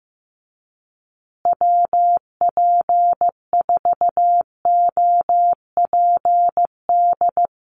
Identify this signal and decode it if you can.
Morse code, heard as a keyed tone, about 15 wpm, 700 Hz. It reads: WP4OPD
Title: USA Call Signs Repeated 2x 15wpm